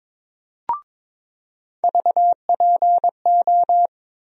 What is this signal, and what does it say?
Morse code, heard as a keyed tone, about 22 wpm, 700 Hz. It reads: VPO
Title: USA Call Sign Postfixes - 3 Letters 22wpm